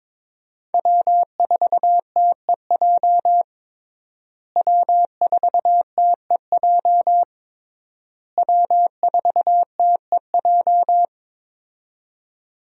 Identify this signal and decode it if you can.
Morse code, heard as a keyed tone, about 21 wpm, 700 Hz. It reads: W4TEJ W4TEJ W4TEJ
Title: USA Call Signs - 5 Characters - Repeated 3x - 22wpm